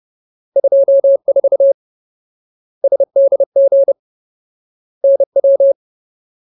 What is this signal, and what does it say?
Morse code, heard as a keyed tone, about 30 wpm, 550 Hz. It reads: 24 SDG NW
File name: Ultimate Word List / 30WPM Ultimate